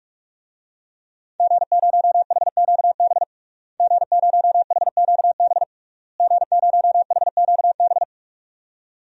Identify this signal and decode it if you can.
Morse code, heard as a keyed tone, about 43 wpm, 700 Hz. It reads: G0HQB G0HQB G0HQB
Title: DXCC-100 Call Signs - 5 Characters - Repeated 3x - 45wpm